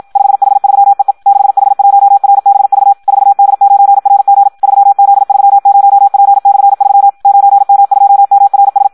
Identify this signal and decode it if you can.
Morse code, heard as a keyed tone, about 56 wpm, 800 Hz. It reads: 5H3EE 6H1RBV 4D9RG 3Z30P/3 9M2MRS
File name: Audio practica CW / qrq-tx-54 2